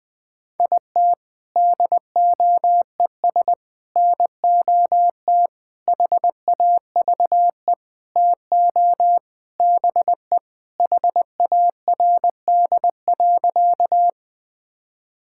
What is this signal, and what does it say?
Morse code, heard as a keyed tone, about 20 wpm, 700 Hz. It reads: IT DOES NOT HAVE TO BE HARD.
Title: Sentences with 4-Word Phrases - Spoken Beginning - 20wpm